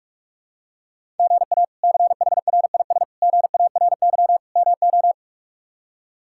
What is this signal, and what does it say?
Morse code, heard as a keyed tone, about 46 wpm, 700 Hz. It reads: GA CHRIS GARY MO
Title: K1USN SST QSO Practice - 45wpm